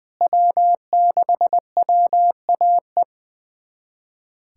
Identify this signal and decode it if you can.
Morse code, heard as a keyed tone, about 20 wpm, 690 Hz. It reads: W6WAE
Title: USA Callsign DE Callsign 20wpm